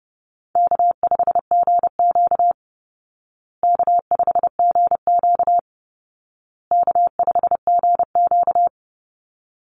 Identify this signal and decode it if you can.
Morse code, heard as a keyed tone, about 28 wpm, 700 Hz. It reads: K5GQ K5GQ K5GQ
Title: Easy Call Signs Repeated 3x - 30wpm